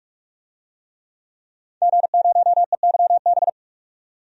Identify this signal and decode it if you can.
Morse code, heard as a keyed tone, about 44 wpm, 700 Hz. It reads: G0EYB
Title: DXCC-100 Call Signs - 5 Characters - 45wpm